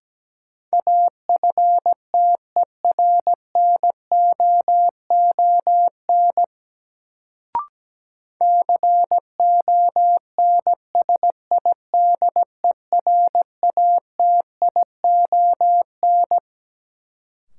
Morse code, heard as a keyed tone, about 17 wpm, 700 Hz. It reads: AFTERNOON CONSIDERATION
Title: Sentences from Top 1500 Words - New Word Drill - 17wpm